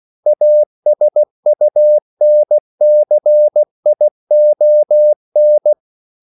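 Morse code, heard as a keyed tone, about 16 wpm, 600 Hz. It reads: ASUNCION
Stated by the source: CW practice of national capitals at 16 words per minute, from ON6ZQ.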